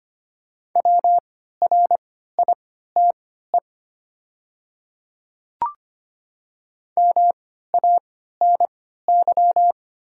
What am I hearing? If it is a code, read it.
Morse code, heard as a keyed tone, about 25 wpm, spaced as for 14 wpm, 690 Hz. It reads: WRITE MANY
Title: 3x Character Spacing - Top 100 Words - 25wpm